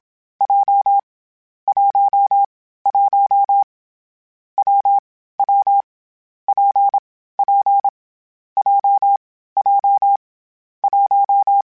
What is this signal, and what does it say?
Morse code, heard as a keyed tone, about 27 wpm, 800 Hz. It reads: J 1 1 W W P P J J 1
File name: Difficult Combos / Difficult Combos-mp3s / WPJ WPJ1 @25